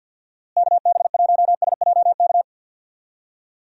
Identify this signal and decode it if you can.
Morse code, heard as a keyed tone, about 48 wpm, 700 Hz. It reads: KB1SJK
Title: USA Call Signs - 6 Characters - Repeated 2x - 50wpm